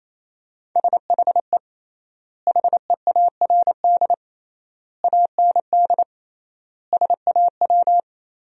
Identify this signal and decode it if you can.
Morse code, heard as a keyed tone, about 28 wpm, 700 Hz. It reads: SHE HEARD AND SAW
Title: 3x Word-Spacing - Sentences from Top 500 Words - 28wpm